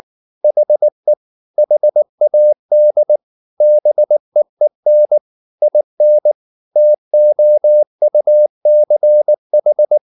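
Morse code, heard as a keyed tone, about 19 wpm, 600 Hz. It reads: HE HAD BEEN IN TOUCH
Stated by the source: Morse code at 19 wpm from CW med Gnister, reading Ken Follett's novel Never in English.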